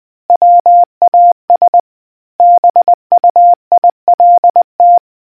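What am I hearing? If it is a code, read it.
Morse code, heard as a keyed tone, about 20 wpm, 700 Hz. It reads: WAS BUILT
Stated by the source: Morse quiz question from the RNZN Communicators Association.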